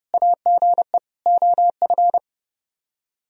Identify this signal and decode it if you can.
Morse code, heard as a keyed tone, about 36 wpm, 700 Hz. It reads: AGE OF
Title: Speed-Racing - Sets of 4 Words 30wpm